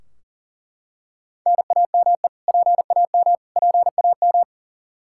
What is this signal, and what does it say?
Morse code, heard as a keyed tone, about 40 wpm, 700 Hz. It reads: NAME PAM PAM
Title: Practice QSO - Exchange 1 - Beginning - 40wpm